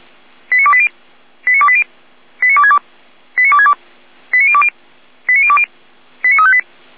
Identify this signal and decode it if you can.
Five-tone selective call (ZVEI), heard as quick runs of five tones, tones of about 70 ms. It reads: 89280 89280 89262 89262 89020 89020 89358
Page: SDR/ZVEI